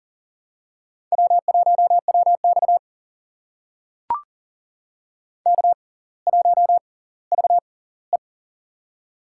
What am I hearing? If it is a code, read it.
Morse code, heard as a keyed tone, about 40 wpm, spaced as for 20 wpm, 700 Hz. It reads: W1WX K1VE
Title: USA Call Signs - 6x Character Spacing - 40wpm